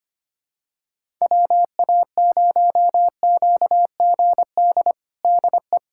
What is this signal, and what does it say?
Morse code, heard as a keyed tone, about 25 wpm, 700 Hz. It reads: WA0QGD DE
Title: USA Callsign DE Callsign 25wpm